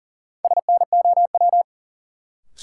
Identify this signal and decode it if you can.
Morse code, heard as a keyed tone, about 40 wpm, 700 Hz. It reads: SNOW
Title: Sentences from Top 500 Words - New Word Drill - 40wpm